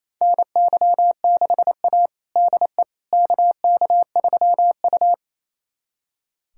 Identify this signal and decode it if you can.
Morse code, heard as a keyed tone, about 28 wpm, 700 Hz. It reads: NY6A DE KK3U
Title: USA Callsign DE Callsign - 4 Characters - 28wpm